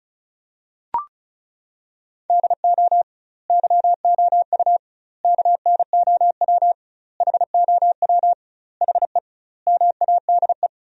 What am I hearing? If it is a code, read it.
Morse code, heard as a keyed tone, about 34 wpm, 700 Hz. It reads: DO YOU KNOW HOW HE MADE
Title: 2x Word-Spacing - Sentences from Top 300 Words - 35wpm